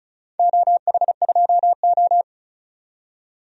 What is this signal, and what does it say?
Morse code, heard as a keyed tone, about 36 wpm, 700 Hz. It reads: OH2O
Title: DXCC-100 Call Signs - 4 Characters - 35wpm